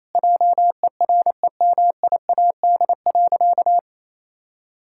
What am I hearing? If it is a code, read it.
Morse code, heard as a keyed tone, about 28 wpm, 700 Hz. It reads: JEREMIAD.
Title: Pangram Challenge v2 - 28wpm